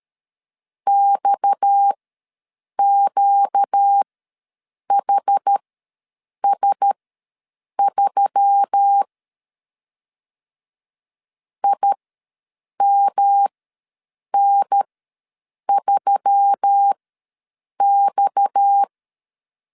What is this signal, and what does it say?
Morse code, heard as a keyed tone, about 13 wpm, spaced as for 7 wpm, 800 Hz. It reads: XQHS3 IMN3X
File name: LEZIONI / 6 PKX0 / 6 PKX ws 12wpm 3106